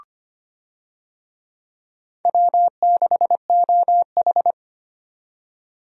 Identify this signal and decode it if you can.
Morse code, heard as a keyed tone, about 25 wpm, 700 Hz. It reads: W6OH
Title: USA Call Signs - 4 Characters - 25wpm